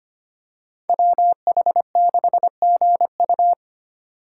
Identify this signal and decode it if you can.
Morse code, heard as a keyed tone, about 25 wpm, 700 Hz. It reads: WH6GU
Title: USA Call Signs - 2x Character Spacing - 25wpm